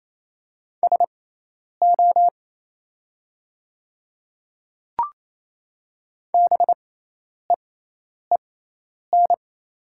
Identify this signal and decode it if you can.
Morse code, heard as a keyed tone, about 27 wpm, spaced as for 10 wpm, 700 Hz. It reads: SO BEEN
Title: 6x Character Spacing - Top 100 Words - 28wpm